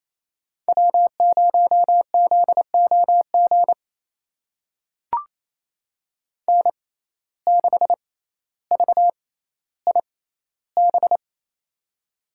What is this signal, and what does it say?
Morse code, heard as a keyed tone, about 28 wpm, spaced as for 14 wpm, 700 Hz. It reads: W0ZOG N6VIB
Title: USA Call Signs - 5 Characters - 6x Character Spacing - 28wpm